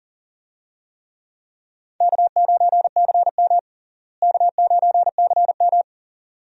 Morse code, heard as a keyed tone, about 40 wpm, 700 Hz. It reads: K9CM K9CM
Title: USA Call Signs - 4 Characters - Repeated 2x - 40wpm